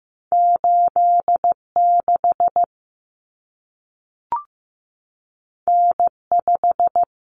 Morse code, heard as a keyed tone, about 15 wpm, 700 Hz. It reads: 86 N5
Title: Common Cut Numbers - 2-Digit 15wpm